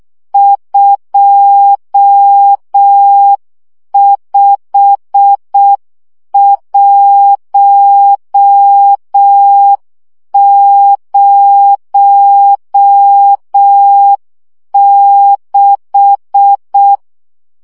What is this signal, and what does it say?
Morse code, heard as a keyed tone, about 6 wpm, 800 Hz. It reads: 25106
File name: morze vizsga 2011-2 5